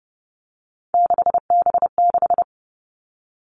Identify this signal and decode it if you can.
Morse code, heard as a keyed tone, about 30 wpm, 700 Hz. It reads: 6B6
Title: All Cut Numbers - 3-Digit 30wpm